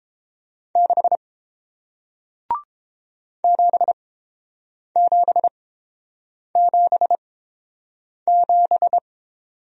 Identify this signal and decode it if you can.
Morse code, heard as a keyed tone, about 30 wpm, spaced as for 7 wpm, 700 Hz. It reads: 6 7777